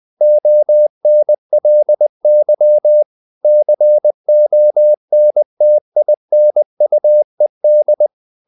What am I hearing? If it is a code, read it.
Morse code, heard as a keyed tone, about 20 wpm, 600 Hz. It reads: ONLY CONTINUED
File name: the old man and the sea / the old man and the sea part 1 20wpm